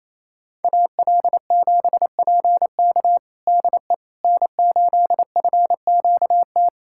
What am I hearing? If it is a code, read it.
Morse code, heard as a keyed tone, about 28 wpm, 700 Hz. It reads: AL7PK DE N8FQT
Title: USA Callsign DE Callsign - 5 Characters - 28wpm